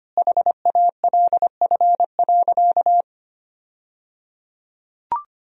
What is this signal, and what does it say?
Morse code, heard as a keyed tone, about 25 wpm, 700 Hz. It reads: HALF.
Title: 3x Word-Spacing - Sentences from Top 400 Words - 25wpm